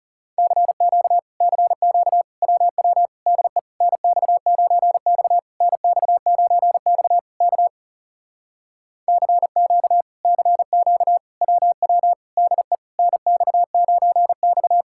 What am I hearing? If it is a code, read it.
Morse code, heard as a keyed tone, about 38 wpm, 700 Hz. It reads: CQ CQ WW DE NX9X NX9X K CQ CQ WW DE NX9X
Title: Speed-Racing Common QSO Phrases Encore - 30wpm